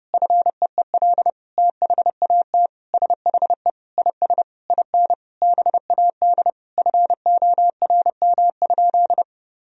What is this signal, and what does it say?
Morse code, heard as a keyed tone, about 32 wpm, 700 Hz. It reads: FEEL THAT SHE IS IN BAD FORM?
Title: Sentences from Top 1000 Words - Review Part 1 - 30wpm